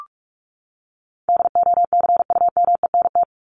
Morse code, heard as a keyed tone, about 44 wpm, 700 Hz. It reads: DOCUMENT
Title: Top 1751-2000 Words 45wpm